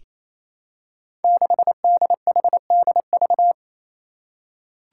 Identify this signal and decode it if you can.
Morse code, heard as a keyed tone, about 28 wpm, 700 Hz. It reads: 6DHDV
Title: Five Challenging Letter-Number Combinations 28wpm